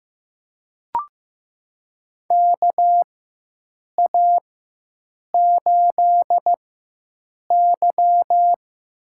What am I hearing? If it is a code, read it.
Morse code, heard as a keyed tone, about 15 wpm, spaced as for 7 wpm, 700 Hz. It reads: KA8Y
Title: USA Call Signs - 4x Character Spacing - 15wpm